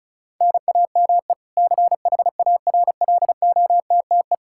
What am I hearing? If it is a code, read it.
Morse code, heard as a keyed tone, about 36 wpm, 700 Hz. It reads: NAME CHARLOTTE
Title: Practice QSO - Exchange 1 - Beginning - 35wpm